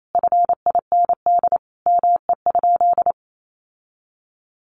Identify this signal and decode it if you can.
Morse code, heard as a keyed tone, about 28 wpm, 700 Hz. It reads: FIND ME?
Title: Sentences from Top 1400 Words 28wpm